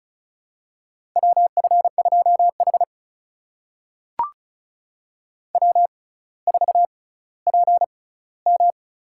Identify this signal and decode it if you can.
Morse code, heard as a keyed tone, about 36 wpm, spaced as for 18 wpm, 700 Hz. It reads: WF2H W4PM
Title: USA Call Signs - 4 Characters - 6x Character Spacing - 35wpm